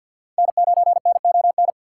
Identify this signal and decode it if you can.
Morse code, heard as a keyed tone, about 47 wpm, 700 Hz. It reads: N9NON
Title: USA Call Signs - 5 Characters - 6x Character Spacing - 50wpm